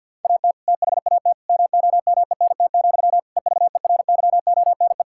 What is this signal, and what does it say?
Morse code, heard as a keyed tone, about 51 wpm, 700 Hz. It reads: AT THAT MOMENT, EVERYONE
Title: Sentences from Top 1000 Words - Review Part 5 - 50wpm